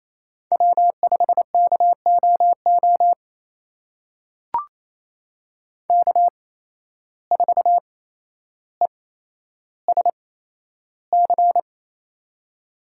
Morse code, heard as a keyed tone, about 28 wpm, spaced as for 12 wpm, 700 Hz. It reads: W5KOO K4ESC
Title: USA Call Signs - 5 Characters - 8x Character Spacing - 28wpm